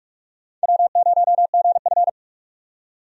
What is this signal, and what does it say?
Morse code, heard as a keyed tone, about 43 wpm, 700 Hz. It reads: W0GF
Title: USA Call Signs - 4 Characters - 45wpm